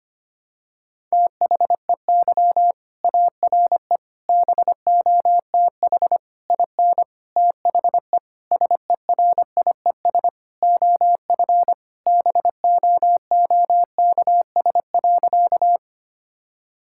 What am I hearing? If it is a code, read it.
Morse code, heard as a keyed tone, about 25 wpm, 700 Hz. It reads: THEY ARE BOTH IN THE SERIES OF BOOKS.
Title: Sentences from Top 1000 Words - Review Part 5 - 25wpm